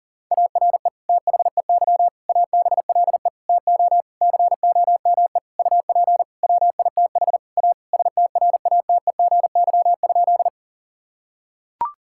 Morse code, heard as a keyed tone, about 37 wpm, 700 Hz. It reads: ARE THEY ABLE TO COME UP WITH A STRATEGY?
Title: Sentences from Top 1000 Words - Review Part 5 - 40wpm